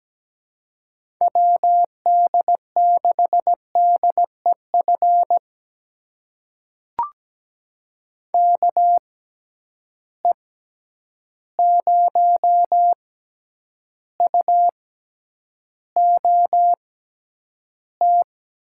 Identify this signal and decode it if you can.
Morse code, heard as a keyed tone, about 17 wpm, spaced as for 9 wpm, 700 Hz. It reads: WD6DEF KE0UOT
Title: USA Call Signs - 6 Characters - 6x Character Spacing - 17wpm